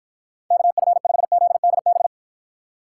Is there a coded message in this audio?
Yes: Morse code from 0.5 to 2.1 s, heard as a keyed tone, about 53 wpm, 700 Hz. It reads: KF5ZDB